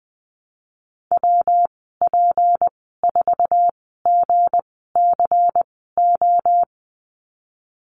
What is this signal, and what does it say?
Morse code, heard as a keyed tone, about 19 wpm, spaced as for 14 wpm, 700 Hz. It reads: WP4GCO
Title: 207 Lesson 33g - Letter j - Call Signs 20wpm